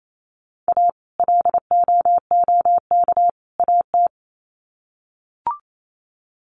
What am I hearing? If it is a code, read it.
Morse code, heard as a keyed tone, about 28 wpm, 700 Hz. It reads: A LOOK AT
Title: Sets of 3 Words 28wpm